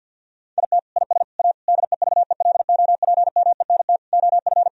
Morse code, heard as a keyed tone, about 50 wpm, 700 Hz. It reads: IT IS A DEVELOPMENT OF
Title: Sentences from Top 1000 Words - Review Part 4 - 50wpm